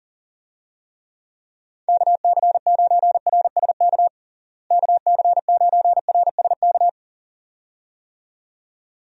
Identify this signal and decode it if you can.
Morse code, heard as a keyed tone, about 40 wpm, 700 Hz. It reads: KC9RSK KC9RSK
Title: USA Call Signs - 6 Characters - Repeated 2x - 40wpm